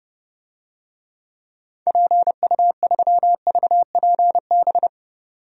Morse code, heard as a keyed tone, about 30 wpm, 700 Hz. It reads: PU3VPB
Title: DXCC-100 Call Signs - 6 Characters - 30wpm